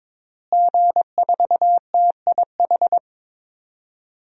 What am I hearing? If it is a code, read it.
Morse code, heard as a keyed tone, about 22 wpm, 700 Hz. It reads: G4TIH
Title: DXCC-100 Call Signs - 5 Characters - Repeated 2x - 22wpm